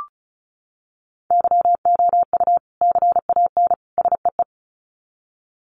Morse code, heard as a keyed tone, about 34 wpm, 700 Hz. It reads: YOU CAN SEE
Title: Sets of 3 Words 35wpm